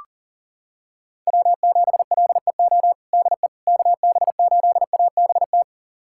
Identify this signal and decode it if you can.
Morse code, heard as a keyed tone, about 40 wpm, 700 Hz. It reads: W7LEO DE KB8ABT
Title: USA Callsign DE Callsign 40wpm